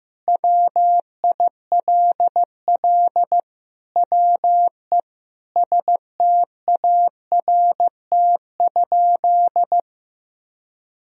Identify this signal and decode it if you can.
Morse code, heard as a keyed tone, about 15 wpm, 700 Hz. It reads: WILL WE START?
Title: Sentences from Top 1000 Words - Review Part 1 - 15wpm